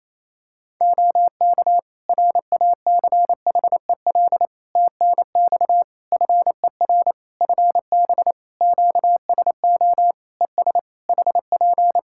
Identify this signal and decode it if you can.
Morse code, heard as a keyed tone, about 28 wpm, 700 Hz. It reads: OK RACHEL TNX FER FB QSO ES HP
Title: K1USN SST QSO Practice 28wpm